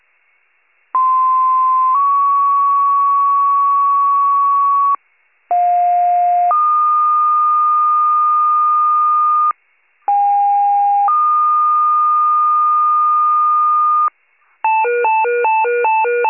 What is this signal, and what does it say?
Two-tone page, heard as a one-second tone, then a three-second tone, about 1040 then 1120 Hz, about 710 then 1220 Hz, about 810 then 1220 Hz. It is heard from 0.9 to 14.1 s.